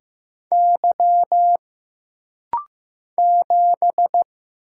Morse code, heard as a keyed tone, about 15 wpm, 700 Hz. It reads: Y 7
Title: Single Letter-Number Alt - Rapid-Fire 15wpm